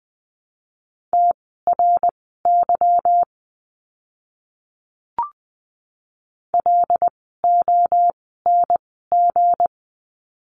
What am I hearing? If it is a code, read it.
Morse code, heard as a keyed tone, about 20 wpm, spaced as for 15 wpm, 700 Hz. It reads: TRY LONG
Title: 160 Lesson 26f - Letter g - Words 20wpm